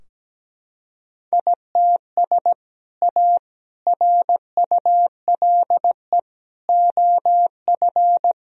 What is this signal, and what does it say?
Morse code, heard as a keyed tone, about 17 wpm, 700 Hz. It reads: ITS A RULE OF
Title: Sentences from Top 1000 Words - Review Part 2 - 17wpm